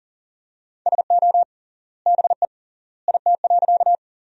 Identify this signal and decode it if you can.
Morse code, heard as a keyed tone, about 40 wpm, 700 Hz. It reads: SO BE IT.
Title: 3x Word-Spacing - Sentences from Top 100 Words - 40wpm